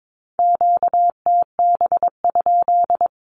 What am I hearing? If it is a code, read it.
Morse code, heard as a keyed tone, about 22 wpm, 700 Hz. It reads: QTB?
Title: Q-Codes Repeated 3x 22wpm